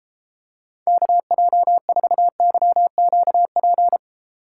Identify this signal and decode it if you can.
Morse code, heard as a keyed tone, about 31 wpm, 700 Hz. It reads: KJ4YQP